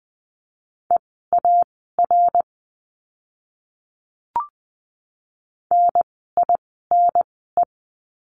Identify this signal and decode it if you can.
Morse code, heard as a keyed tone, about 20 wpm, spaced as for 15 wpm, 700 Hz. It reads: EAR NINE
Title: 054 Lesson 10f - Letter r - Words 20wpm